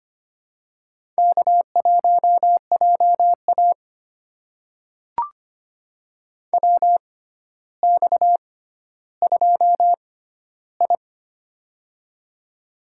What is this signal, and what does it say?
Morse code, heard as a keyed tone, about 25 wpm, spaced as for 13 wpm, 700 Hz. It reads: K1JA WX2I
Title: USA Call Signs - 4 Characters - 6x Character Spacing - 25wpm